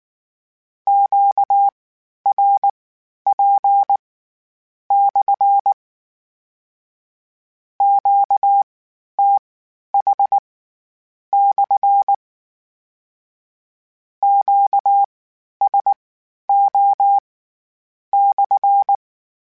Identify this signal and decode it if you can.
Morse code, heard as a keyed tone, about 19 wpm, spaced as for 9 wpm, 800 Hz. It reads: QRP/ QTH/ QSO/
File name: Q Calls / Q Calls-mp3s / Q Calls SPSL1 @18